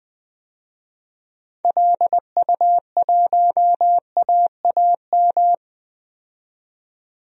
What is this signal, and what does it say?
Morse code, heard as a keyed tone, about 20 wpm, 700 Hz. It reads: LU1AAM
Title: DXCC-100 Call Signs - 6 Characters - 20wpm